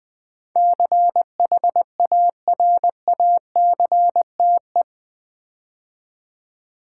Morse code, heard as a keyed tone, about 20 wpm, 700 Hz. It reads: CHARACTE
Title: Build A Word - 4 Letters At A Time - 12-Letter Words - 20wpm